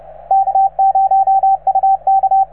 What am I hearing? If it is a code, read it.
Morse code, heard as a keyed tone, about 35 wpm, 770 Hz, with radio noise. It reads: K0UK